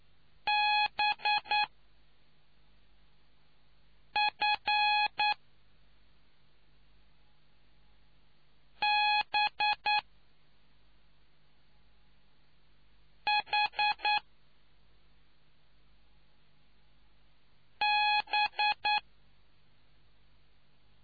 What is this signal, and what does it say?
Morse code, heard as a keyed tone, about 9 wpm, spaced as for 3 wpm, 840 Hz. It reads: BF B H B